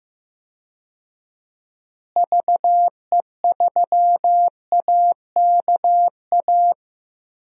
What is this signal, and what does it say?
Morse code, heard as a keyed tone, about 15 wpm, 700 Hz. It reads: VE3AKA
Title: DXCC-100 Call Signs - 6 Characters - Repeated 3x - 15wpm